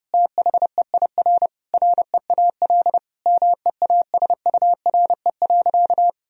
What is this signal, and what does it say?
Morse code, heard as a keyed tone, about 32 wpm, 700 Hz. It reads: THEIR REAL MEASURE.
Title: Sentences from Top 1000 Words - Review Part 1 - 30wpm